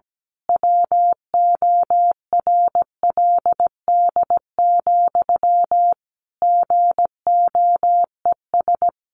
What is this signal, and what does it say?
Morse code, heard as a keyed tone, about 17 wpm, 700 Hz. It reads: WORLD, GOES
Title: A Sound Of Thunder 17wpm